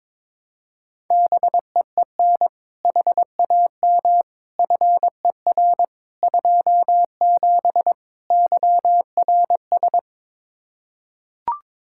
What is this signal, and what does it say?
Morse code, heard as a keyed tone, about 22 wpm, 700 Hz. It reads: BEEN HAM FER 27 YRS
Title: Practice QSO - Exchange 3 - More Info - 22wpm